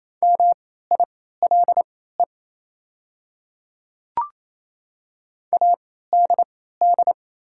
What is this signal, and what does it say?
Morse code, heard as a keyed tone, about 28 wpm, spaced as for 16 wpm, 700 Hz. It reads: MILE ADD
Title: 3x Character Spacing - Top 300 Words - 28wpm